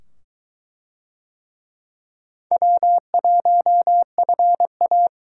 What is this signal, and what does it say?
Morse code, heard as a keyed tone, about 22 wpm, 700 Hz. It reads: W1FA